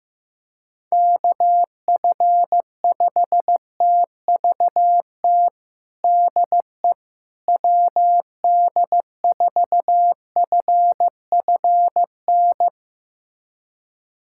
Morse code, heard as a keyed tone, about 15 wpm, 700 Hz. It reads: KF5TVT DE WD4FFN
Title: USA Callsign DE Callsign - 6 Characters - 15wpm